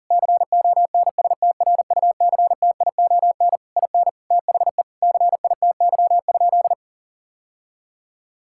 Morse code, heard as a keyed tone, about 40 wpm, 700 Hz. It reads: CONSTRUCTION IN THE CITY?
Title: Sentences from Top 1600 Words 40wpm